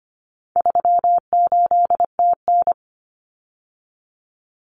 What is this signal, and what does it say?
Morse code, heard as a keyed tone, about 25 wpm, 700 Hz. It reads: 38TN
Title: Common Cut Numbers - 4-Digit 25wpm